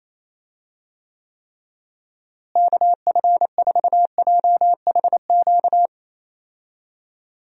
Morse code, heard as a keyed tone, about 28 wpm, 700 Hz. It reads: KF4JHQ
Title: USA Call Signs - 6 Characters - Repeated 3x - 28wpm